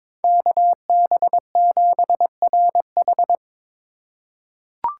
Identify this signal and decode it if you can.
Morse code, heard as a keyed tone, about 22 wpm, 700 Hz. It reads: KB7RH
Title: USA Call Signs - 3x Character Spacing - 22wpm